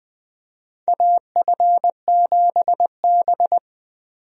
Morse code, heard as a keyed tone, about 20 wpm, 700 Hz. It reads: AF7B
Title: USA Call Signs - 4 Characters - 20wpm